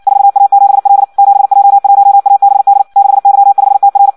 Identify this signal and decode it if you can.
Morse code, heard as a keyed tone, about 56 wpm, 800 Hz. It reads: 4A7L 7J1ABD 6Y5TR
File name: Audio practica CW / qrq-tx-58 2